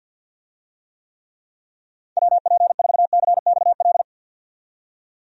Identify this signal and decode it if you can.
Morse code, heard as a keyed tone, about 47 wpm, 700 Hz. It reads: WP4CXL
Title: USA Call Signs - 6 Characters - 50wpm